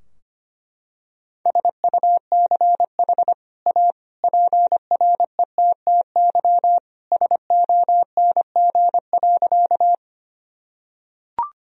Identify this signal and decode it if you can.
Morse code, heard as a keyed tone, about 25 wpm, 700 Hz. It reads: SUCH A PRETTY SONG.
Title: Sentences from Top 1000 Words - Review Part 2 - 25wpm